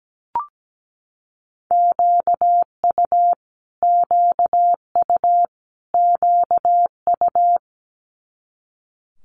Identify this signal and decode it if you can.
Morse code, heard as a keyed tone, about 17 wpm, 700 Hz. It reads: QU QU QU
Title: Multi-letter Phonemes Repeated 3x 17wpm